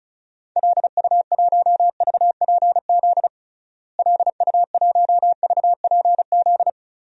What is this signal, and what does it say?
Morse code, heard as a keyed tone, about 34 wpm, 700 Hz. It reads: LU1VPZ LU1VPZ
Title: DXCC-100 Call Signs - 6 Characters - Repeated 2x - 35wpm